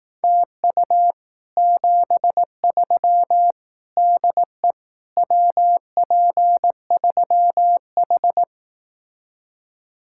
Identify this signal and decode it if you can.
Morse code, heard as a keyed tone, about 18 wpm, 700 Hz. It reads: TU 73 DE WP3H
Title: Speed-Racing Common QSO Phrases Encore - 15wpm